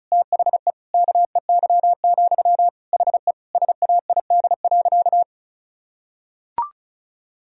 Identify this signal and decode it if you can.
Morse code, heard as a keyed tone, about 34 wpm, 700 Hz. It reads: THE KEY, HE SAID.
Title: Sentences from Top 1000 Words - Review Part 4 - 35wpm